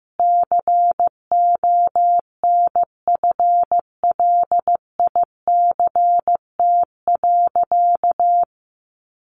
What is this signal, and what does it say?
Morse code, heard as a keyed tone, about 14 wpm, 700 Hz. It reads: CONFLICT.
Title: Sentences from Top 1300 Words 15wpm